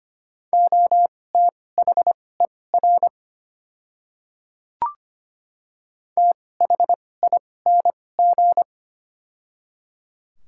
Morse code, heard as a keyed tone, about 25 wpm, spaced as for 18 wpm, 700 Hz. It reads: OTHER THING
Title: 2x Character Spacing - Top 300 Words - 25wpm